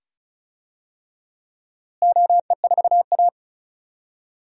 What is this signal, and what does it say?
Morse code, heard as a keyed tone, about 34 wpm, 700 Hz. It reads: OE4A
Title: DXCC-100 Call Signs - 4 Characters - Repeated 2x - 35wpm